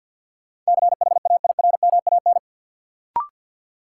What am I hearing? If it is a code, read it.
Morse code, heard as a keyed tone, about 51 wpm, 700 Hz. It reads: CHAIRMAN
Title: Sets of 3 Words - New Word Drill - 50wpm